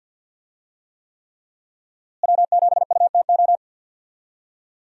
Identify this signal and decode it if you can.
Morse code, heard as a keyed tone, about 52 wpm, 700 Hz. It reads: W7UTO
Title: USA Call Signs - 5 Characters - Repeated 3x - 50wpm